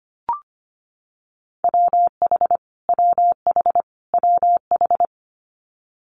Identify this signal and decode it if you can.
Morse code, heard as a keyed tone, about 25 wpm, 700 Hz. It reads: WH WH WH
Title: Multi-letter Phonemes Repeated 3x 25wpm